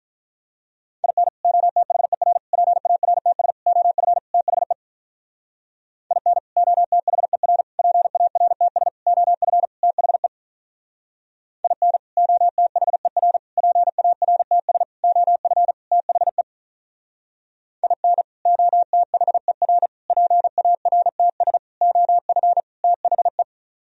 Morse code, heard as a keyed tone, about 44 wpm, 700 Hz. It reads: IN OTHER PARTS OF THE IN OTHER PARTS OF THE IN OTHER PARTS OF THE IN OTHER PARTS OF THE
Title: Speed-Racing - Sets of 5 Words 35wpm